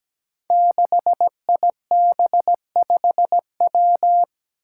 Morse code, heard as a keyed tone, about 17 wpm, 700 Hz. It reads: 6IB5W
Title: Five Challenging Letter-Number Combinations 17wpm